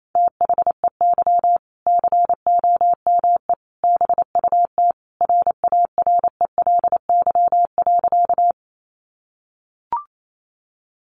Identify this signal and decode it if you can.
Morse code, heard as a keyed tone, about 28 wpm, 700 Hz. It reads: THEY COME BUT RARELY.
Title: Sentences from Top 1400 Words 28wpm